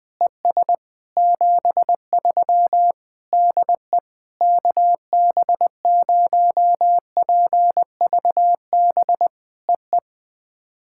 Morse code, heard as a keyed tone, about 20 wpm, 700 Hz. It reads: ES 73 DE KB0PVB EE
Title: POTA QSO Practice - 20wpm